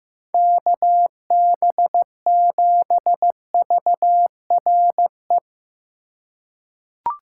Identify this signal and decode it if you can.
Morse code, heard as a keyed tone, about 15 wpm, 700 Hz. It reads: KB7VRE
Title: USA Call Signs - 6 Characters - 4x Character Spacing - 15wpm